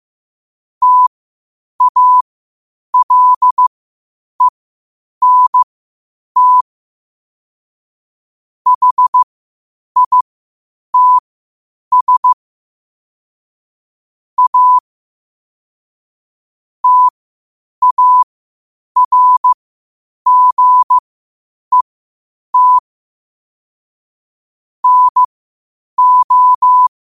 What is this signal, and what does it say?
Morse code, heard as a keyed tone, about 15 wpm, spaced as for 8 wpm, 1000 Hz. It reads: TALENT HITS A TARGET NO
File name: qotdfiles / 8wpm